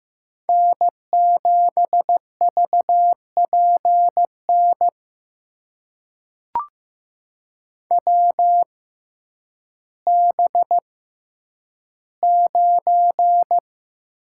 Morse code, heard as a keyed tone, about 15 wpm, spaced as for 9 wpm, 700 Hz. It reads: N7VPN WB9
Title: USA Call Signs - 6x Character Spacing - 15wpm